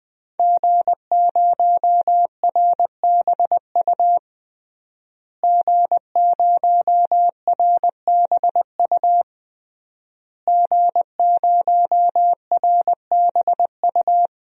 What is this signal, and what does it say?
Morse code, heard as a keyed tone, about 21 wpm, 700 Hz. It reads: G0RBU G0RBU G0RBU
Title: DXCC-100 Call Signs - 5 Characters - Repeated 3x - 20wpm